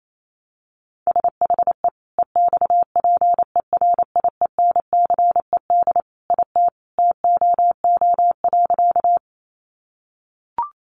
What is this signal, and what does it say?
Morse code, heard as a keyed tone, about 28 wpm, 700 Hz. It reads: SHE EXPERIENCED IT TOO.
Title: Sentences from Top 1100 Words 28wpm